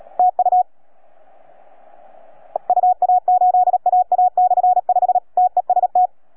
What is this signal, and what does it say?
Morse code, heard as a keyed tone, about 37 wpm, 720 Hz. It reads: TU UA8AA/5 TEST